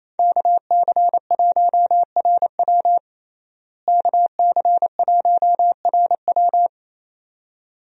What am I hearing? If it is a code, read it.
Morse code, heard as a keyed tone, about 28 wpm, 700 Hz. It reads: KC1RW KC1RW
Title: USA Call Signs - 5 Characters - Repeated 2x - 28wpm